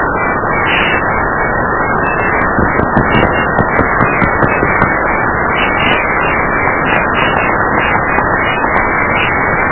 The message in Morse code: none